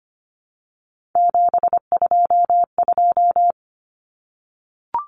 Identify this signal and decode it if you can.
Morse code, heard as a keyed tone, about 25 wpm, 700 Hz. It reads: 722
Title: Common Cut Numbers - 3-Digit 25wpm